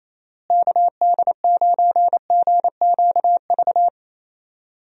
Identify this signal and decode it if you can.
Morse code, heard as a keyed tone, about 28 wpm, 700 Hz. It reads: KD9GQV
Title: USA Call Signs - 6 Characters - Repeated 3x - 28wpm